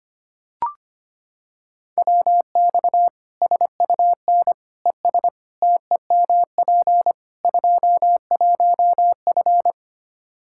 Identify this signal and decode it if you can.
Morse code, heard as a keyed tone, about 25 wpm, 700 Hz. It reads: WX SUN ES TEMP 21F
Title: Practice QSO - Exchange 2 - Info - 25wpm